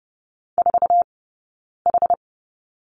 Morse code, heard as a keyed tone, about 30 wpm, 700 Hz. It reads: 4 H
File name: 092 Lesson 16d - Letter c - Two Characters 30wpm